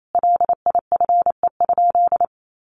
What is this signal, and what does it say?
Morse code, heard as a keyed tone, about 28 wpm, 700 Hz. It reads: LIFE?
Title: Sentences from Top 1300 Words 28wpm